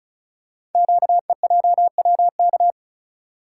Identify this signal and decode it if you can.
Morse code, heard as a keyed tone, about 36 wpm, 700 Hz. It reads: QEJWK
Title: Code Groups - 5 Letters - 35wpm